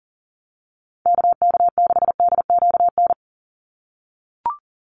Morse code, heard as a keyed tone, about 40 wpm, 700 Hz. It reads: KK6DQN
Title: USA Call Signs 40wpm